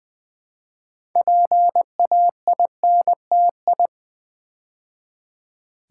Morse code, heard as a keyed tone, about 20 wpm, 700 Hz. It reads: PAINTI
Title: Build A Word - 3 Letters At A Time - 8-Letter Words - 20wpm